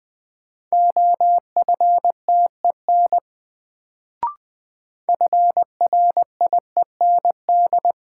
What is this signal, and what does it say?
Morse code, heard as a keyed tone, about 21 wpm, 700 Hz. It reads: OFTEN FRIEND
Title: Sentences from Top 300 Words - New Word Drill - 20wpm